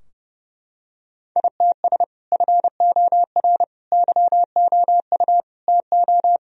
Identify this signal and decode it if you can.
Morse code, heard as a keyed tone, about 30 wpm, 700 Hz. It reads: ITS FOR YOU TO
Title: Sentences from Top 1000 Words - Review Part 3 - 30wpm